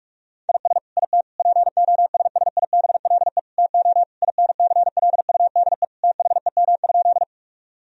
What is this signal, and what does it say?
Morse code, heard as a keyed tone, about 46 wpm, 700 Hz. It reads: IS IT POSSIBLE TO INCLUDE THEM?
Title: Sentences from Top 1000 Words - Review Part 3 - 45wpm